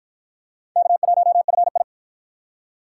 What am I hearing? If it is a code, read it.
Morse code, heard as a keyed tone, about 50 wpm, 700 Hz. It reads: K1FI